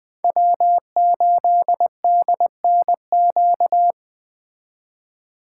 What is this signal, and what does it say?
Morse code, heard as a keyed tone, about 20 wpm, 700 Hz. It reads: W8DNQ
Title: Speed Racing - USA Call Signs 17wpm